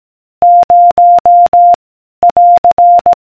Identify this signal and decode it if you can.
Morse code, heard as a keyed tone, about 18 wpm, 700 Hz. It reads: 0 <AR>